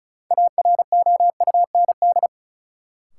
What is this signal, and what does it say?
Morse code, heard as a keyed tone, about 36 wpm, 700 Hz. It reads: AROUND
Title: Sets of 2 Words - New Word Drill - 35wpm